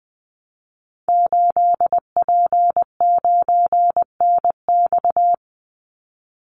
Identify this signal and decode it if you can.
Morse code, heard as a keyed tone, about 20 wpm, 700 Hz. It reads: 8P9NX
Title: Call Signs - 20wpm